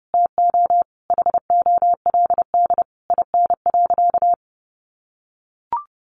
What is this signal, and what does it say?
Morse code, heard as a keyed tone, about 30 wpm, 700 Hz. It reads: TO HOLD IN.
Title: Sentences from Top 400 Words 30wpm